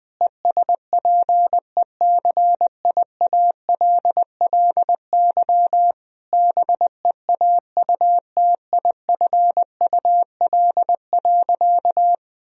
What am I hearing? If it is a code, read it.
Morse code, heard as a keyed tone, about 20 wpm, 700 Hz. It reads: ESPECIALLY BEAUTIFUL.
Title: Sentences from Top 1000 Words - Review Part 4 - 20wpm